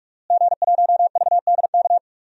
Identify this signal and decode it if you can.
Morse code, heard as a keyed tone, about 46 wpm, 700 Hz. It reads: G1VDK
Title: DXCC-100 Call Signs - 5 Characters - Repeated 3x - 45wpm